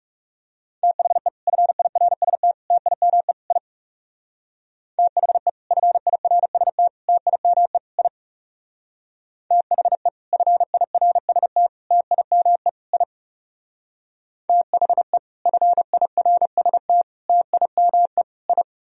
Morse code, heard as a keyed tone, about 37 wpm, 700 Hz. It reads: THE FIRST TIME I THE FIRST TIME I THE FIRST TIME I THE FIRST TIME I
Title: Speed-Racing - Sets of 4 Words 30wpm